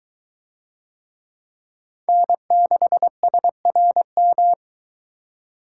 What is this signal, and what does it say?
Morse code, heard as a keyed tone, about 23 wpm, 700 Hz. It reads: N6SRM